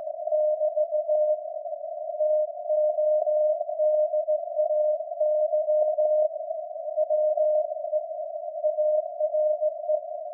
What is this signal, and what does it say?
Morse code, heard as a keyed tone, about 16 wpm, 610 Hz. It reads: <BT> TODAY WE ARE
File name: iz7slz / VLF / SAQ jun 2019